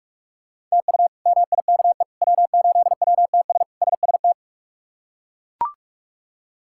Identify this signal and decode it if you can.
Morse code, heard as a keyed tone, about 41 wpm, 700 Hz. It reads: TU MIKE W8WTS SST
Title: K1USN SST QSO Practice - 45wpm